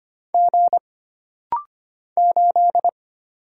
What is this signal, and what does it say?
Morse code, heard as a keyed tone, about 25 wpm, 700 Hz. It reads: G 8
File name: Single Character (Letter:Number:Punctuation:Prosign) - Rapid-Fire - 25wpm